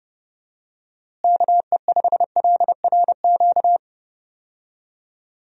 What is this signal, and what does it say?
Morse code, heard as a keyed tone, about 32 wpm, 700 Hz. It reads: KE5LRQ
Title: USA Call Signs - 6 Characters - Repeated 2x - 30wpm